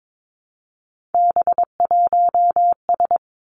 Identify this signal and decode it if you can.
Morse code, heard as a keyed tone, about 22 wpm, 700 Hz. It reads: B1S
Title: Three Challenging Letter-Number Combinations 22wpm